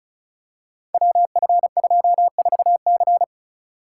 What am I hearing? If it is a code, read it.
Morse code, heard as a keyed tone, about 36 wpm, 700 Hz. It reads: WF24C
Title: Code Groups - 5 Letter:Numbers - 35wpm